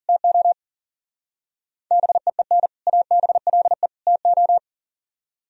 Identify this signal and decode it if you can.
Morse code, heard as a keyed tone, about 49 wpm, 700 Hz. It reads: TO BEEN ABLE TO
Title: Speed-Racing - Sets of 3 Words 40wpm